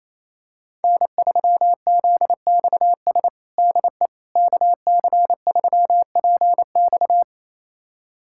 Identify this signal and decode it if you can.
Morse code, heard as a keyed tone, about 28 wpm, 700 Hz. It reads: N3ZXS DE KC3PX
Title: USA Callsign DE Callsign - 5 Characters - 28wpm